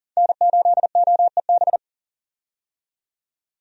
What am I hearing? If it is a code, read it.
Morse code, heard as a keyed tone, about 40 wpm, 700 Hz. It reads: N8OEB
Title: USA Call Signs - 5 Characters - 40wpm